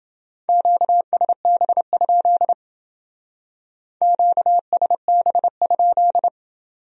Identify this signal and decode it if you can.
Morse code, heard as a keyed tone, about 29 wpm, 700 Hz. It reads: QSB? QSB?